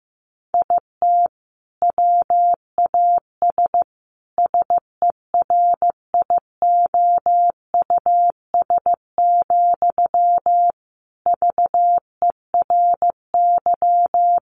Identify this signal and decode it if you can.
Morse code, heard as a keyed tone, about 15 wpm, 700 Hz. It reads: IT WAS SERIOUS, VERY
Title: Sentences from Top 1100 Words 15wpm